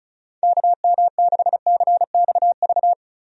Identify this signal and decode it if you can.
Morse code, heard as a keyed tone, about 36 wpm, 700 Hz. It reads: KM6CXV
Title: USA Call Signs - 6 Characters - Repeated 3x - 35wpm